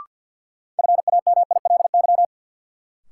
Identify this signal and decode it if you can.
Morse code, heard as a keyed tone, about 52 wpm, 700 Hz. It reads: FAMILY
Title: Sentences from Top 400 Words - New Word Drill - 50wpm